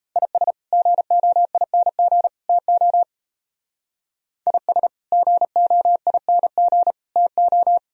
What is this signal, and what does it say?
Morse code, heard as a keyed tone, about 32 wpm, 700 Hz. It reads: IS GOING TO IS GOING TO
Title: Speed-Racing - Sets of 3 Words 28wpm